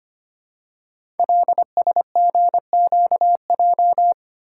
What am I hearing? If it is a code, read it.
Morse code, heard as a keyed tone, about 25 wpm, 700 Hz. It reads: LSGQJ
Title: Code Groups - 5 Letter:Numbers - 25wpm